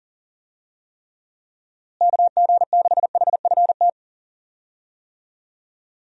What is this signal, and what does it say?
Morse code, heard as a keyed tone, about 37 wpm, 700 Hz. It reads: KG6HFT
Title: USA Call Signs - 6 Characters - Repeated 3x - 40wpm